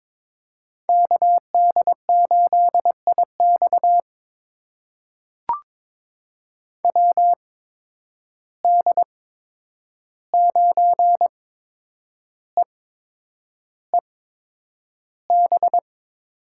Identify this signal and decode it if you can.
Morse code, heard as a keyed tone, about 22 wpm, spaced as for 9 wpm, 700 Hz. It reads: KD8IX WD9EEB
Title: USA Call Signs - 8x Character Spacing - 22wpm